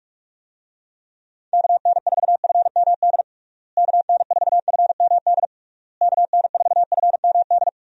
Morse code, heard as a keyed tone, about 46 wpm, 700 Hz. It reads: KN4FMD KN4FMD KN4FMD
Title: USA Call Signs - 6 Characters - Repeated 3x - 45wpm